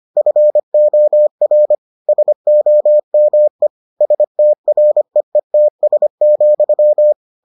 Morse code, heard as a keyed tone, about 25 wpm, 600 Hz. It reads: FOR SOME STREETS,